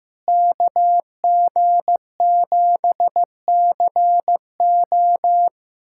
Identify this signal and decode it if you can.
Morse code, heard as a keyed tone, about 15 wpm, 700 Hz. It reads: KG7CO
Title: USA Callsign DE Callsign - 5 Characters - 15wpm